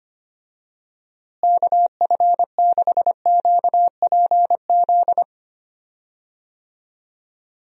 Morse code, heard as a keyed tone, about 25 wpm, 700 Hz. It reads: KF6QPZ
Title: USA Call Signs - 6 Characters - Repeated 3x - 25wpm